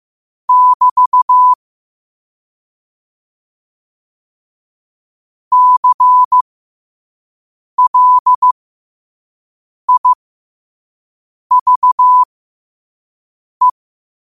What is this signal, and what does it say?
Morse code, heard as a keyed tone, about 15 wpm, spaced as for 5 wpm, 1000 Hz. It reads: <BT> CLIVE
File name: qotdfiles / 5wpm